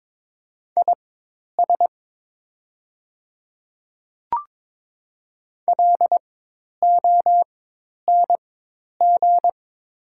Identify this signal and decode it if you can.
Morse code, heard as a keyed tone, about 22 wpm, spaced as for 10 wpm, 700 Hz. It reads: IS LONG
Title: 4x Character Spacing - Top 500 Words - 22wpm